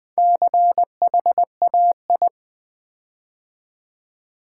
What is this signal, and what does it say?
Morse code, heard as a keyed tone, about 20 wpm, 700 Hz. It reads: CHAI
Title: Build A Word - 4 Letters At A Time - 8-Letter Words - 20wpm